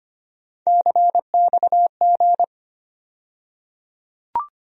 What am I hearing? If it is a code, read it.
Morse code, heard as a keyed tone, about 25 wpm, 700 Hz. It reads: CXG
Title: USA Call Sign Postfixes - 3 Letters 25wpm